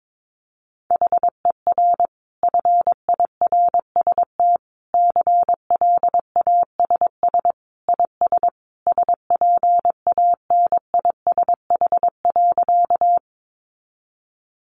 Morse code, heard as a keyed tone, about 22 wpm, 700 Hz. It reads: HER FIRST CLASS IS SPANISH.
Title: Additional Easy Phrases 22wpm